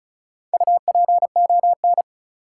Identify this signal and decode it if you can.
Morse code, heard as a keyed tone, about 36 wpm, 700 Hz. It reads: UPON
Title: Sentences from Top 1000 Words - Review Part 6 - New Word Drill - 35wpm